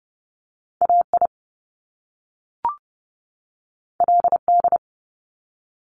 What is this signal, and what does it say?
Morse code, heard as a keyed tone, about 30 wpm, 700 Hz. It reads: AI LD
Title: Top Two Letter Combinations in Words 30wpm